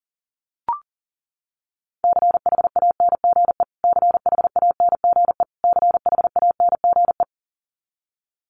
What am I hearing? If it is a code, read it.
Morse code, heard as a keyed tone, about 40 wpm, 700 Hz. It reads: CHANGE CHANGE CHANGE
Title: Top 300 Words Repeated 40wpm